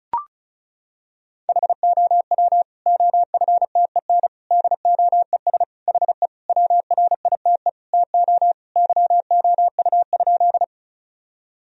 Morse code, heard as a keyed tone, about 36 wpm, 700 Hz. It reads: HOW OFTEN DOES HE WRITE TO YOU?
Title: Sentences from Top 1000 Words - Review Part 1 - 35wpm